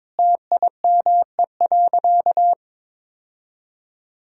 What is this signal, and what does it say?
Morse code, heard as a keyed tone, about 22 wpm, 700 Hz. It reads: TIME.
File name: practice / 22wpm / Sentences with 4-Word Phrases - 22wpm